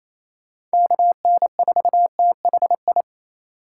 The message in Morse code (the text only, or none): KN4THI